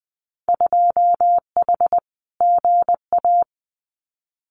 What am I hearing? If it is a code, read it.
Morse code, heard as a keyed tone, about 20 wpm, 700 Hz. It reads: 2H GA
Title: Winter Field Day Exchanges 20wpm